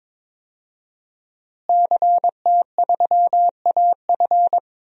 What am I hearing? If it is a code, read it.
Morse code, heard as a keyed tone, about 22 wpm, 700 Hz. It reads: CT3AF
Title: DXCC-100 Call Signs - 5 Characters - Repeated 2x - 22wpm